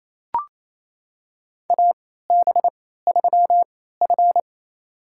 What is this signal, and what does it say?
Morse code, heard as a keyed tone, about 28 wpm, spaced as for 16 wpm, 700 Hz. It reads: AB3F
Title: USA Call Signs - 4 Characters - 3x Character Spacing - 28wpm